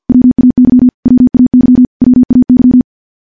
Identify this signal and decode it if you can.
Morse code, heard as a keyed tone, about 50 wpm, 250 Hz. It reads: WAY WAY WAY
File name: morse / morse-training-50WPM